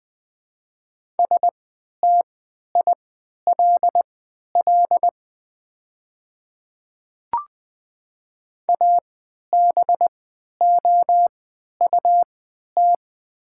Morse code, heard as a keyed tone, about 20 wpm, spaced as for 11 wpm, 700 Hz. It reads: STILL ABOUT
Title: 3x Character Spacing - Top 300 Words - 20wpm